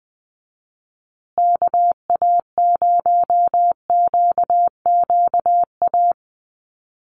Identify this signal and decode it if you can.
Morse code, heard as a keyed tone, about 20 wpm, 700 Hz. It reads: KA0QQA
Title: USA Call Signs Repeated 3x 20wpm